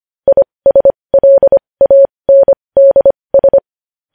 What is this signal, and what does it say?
Morse code, heard as a keyed tone, about 25 wpm, spaced as for 20 wpm, 550 Hz. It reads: ISLANDS
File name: podcast / News headlines-1x-25x20